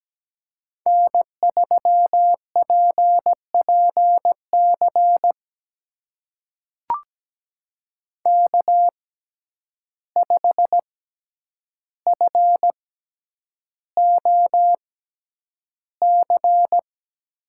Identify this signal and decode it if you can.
Morse code, heard as a keyed tone, about 17 wpm, spaced as for 9 wpm, 700 Hz. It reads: N3PPC K5FOC
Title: USA Call Signs - 5 Characters - 6x Character Spacing - 17wpm